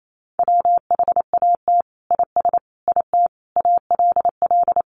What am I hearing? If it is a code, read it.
Morse code, heard as a keyed tone, about 28 wpm, 700 Hz. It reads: WHAT IS IT ALL
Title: Sentences from Top 100 Words 28wpm